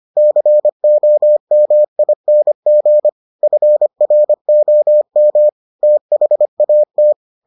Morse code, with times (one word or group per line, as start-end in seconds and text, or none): 0.2-3.1 COMING
3.4-5.5 FROM
5.8-7.1 THAT